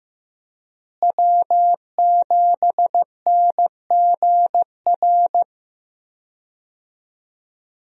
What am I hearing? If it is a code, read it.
Morse code, heard as a keyed tone, about 15 wpm, 700 Hz. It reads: W7NGR
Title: USA Call Signs - 5 Characters - Repeated 2x - 15wpm